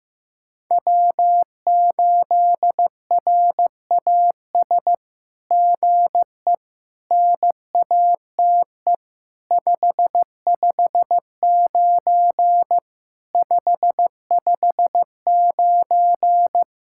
Morse code, heard as a keyed tone, about 15 wpm, 700 Hz. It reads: W8RAS GE NATE 559 559
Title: SOTA QSO Practice 15wpm